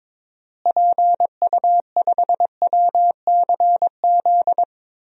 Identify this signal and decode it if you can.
Morse code, heard as a keyed tone, about 22 wpm, 700 Hz. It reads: PU5WCZ
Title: DXCC-100 Call Signs - 6 Characters - Repeated 3x - 22wpm